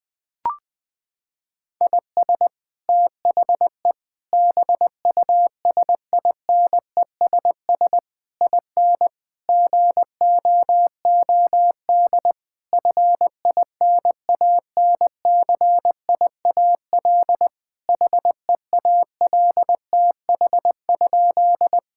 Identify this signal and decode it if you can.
Morse code, heard as a keyed tone, about 21 wpm, 700 Hz. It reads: IS THE BUSINESS IN GOOD FINANCIAL HEALTH?
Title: Sentences from Top 1000 Words - Review Part 5 - 20wpm